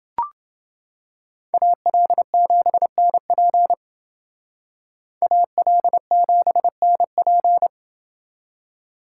Morse code, heard as a keyed tone, about 29 wpm, 700 Hz. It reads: AL7NP AL7NP